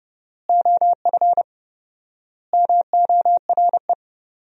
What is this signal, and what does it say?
Morse code, heard as a keyed tone, about 31 wpm, 700 Hz. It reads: OF MORE
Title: 4x Word-Spacing - Sentences from Top 400 Words - 30wpm